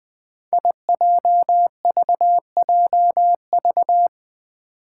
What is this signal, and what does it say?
Morse code, heard as a keyed tone, about 20 wpm, 700 Hz. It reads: IJVJV
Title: Five Challenging Letter-Number Combinations 20wpm